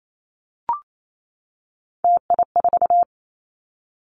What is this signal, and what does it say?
Morse code, heard as a keyed tone, about 28 wpm, 700 Hz. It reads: TI4
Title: Callsign Prefixes for Top 100 DXCC - 28wpm